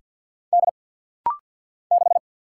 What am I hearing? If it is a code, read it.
Morse code, heard as a keyed tone, about 49 wpm, 700 Hz. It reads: D 6
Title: Single Letter-Number Alt - Rapid-Fire 50wpm